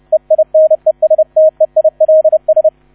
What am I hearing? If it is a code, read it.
Morse code, heard as a keyed tone, about 30 wpm, 630 Hz, with radio noise. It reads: EINESTEILS